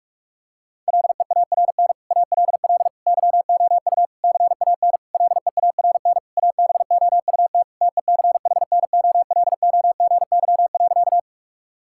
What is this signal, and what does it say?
Morse code, heard as a keyed tone, about 41 wpm, 700 Hz. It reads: LEARN ALL YOU CAN LEARN ABOUT TECHNOLOGY.
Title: Sentences from Top 1000 Words - Review Part 5 - 45wpm